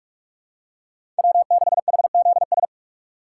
Morse code, heard as a keyed tone, about 46 wpm, 700 Hz. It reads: W6HZS
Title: USA Call Signs - 5 Characters - 3x Character Spacing - 45wpm